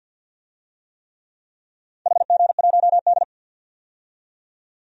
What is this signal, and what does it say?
Morse code, heard as a keyed tone, about 46 wpm, 700 Hz. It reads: HG1D
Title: DXCC-100 Call Signs - 4 Characters - 50wpm